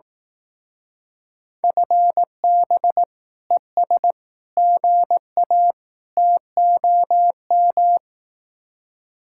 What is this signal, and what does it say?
Morse code, heard as a keyed tone, about 18 wpm, 700 Hz. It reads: FB ES GA TOM
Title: Speed-Racing Common QSO Phrases Encore - 15wpm